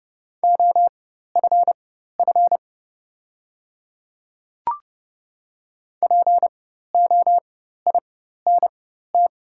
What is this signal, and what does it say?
Morse code, heard as a keyed tone, about 30 wpm, spaced as for 14 wpm, 700 Hz. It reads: OFF POINT
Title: 4x Character Spacing - Top 200 Words - 30wpm